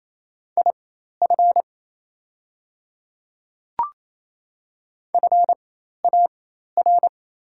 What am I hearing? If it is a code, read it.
Morse code, heard as a keyed tone, about 27 wpm, spaced as for 13 wpm, 700 Hz. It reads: IF FAR
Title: 4x Character Spacing - Top 300 Words - 28wpm